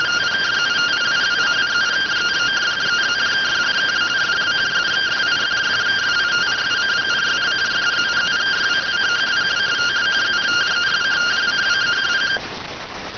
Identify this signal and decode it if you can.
Two-tone teleprinter signal, about 170 Hz shift.